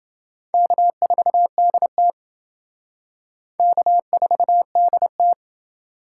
Morse code, heard as a keyed tone, about 28 wpm, 700 Hz. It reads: K4DT K4DT